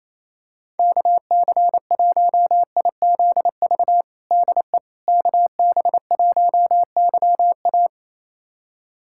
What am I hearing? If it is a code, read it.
Morse code, heard as a keyed tone, about 28 wpm, 700 Hz. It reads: KC1IZV DE KB1YA
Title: USA Callsign DE Callsign 28wpm